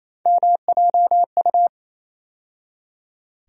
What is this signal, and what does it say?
Morse code, heard as a keyed tone, about 28 wpm, 700 Hz. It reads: MJU
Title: USA Call Sign Postfixes - 3 Letters 28wpm